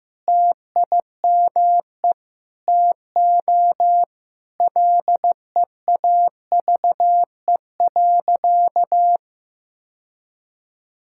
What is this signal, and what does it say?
Morse code, heard as a keyed tone, about 15 wpm, 700 Hz. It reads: TIME TO LEAVE.
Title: Sentences from Top 1000 Words - Review Part 5 - 15wpm